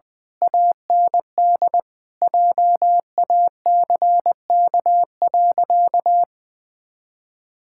Morse code, heard as a keyed tone, about 20 wpm, 700 Hz. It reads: AND JACK.
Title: Pangram Challenge v2 - 20wpm